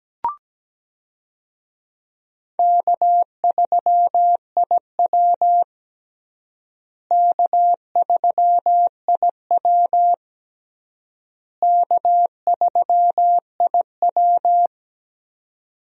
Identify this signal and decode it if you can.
Morse code, heard as a keyed tone, about 17 wpm, 700 Hz. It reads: K3IW K3IW K3IW
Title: USA Call Signs - 4 Characters - Repeated 3x - 17wpm